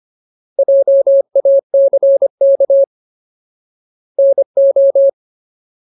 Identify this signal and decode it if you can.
Morse code, heard as a keyed tone, about 25 wpm, 550 Hz. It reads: JACK NO
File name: Ultimate Word List / 25WPM Ultimate